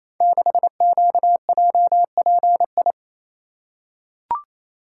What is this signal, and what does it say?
Morse code, heard as a keyed tone, about 27 wpm, 700 Hz. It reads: 6QJPI
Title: Code Groups - 5 Letter:Numbers - 28wpm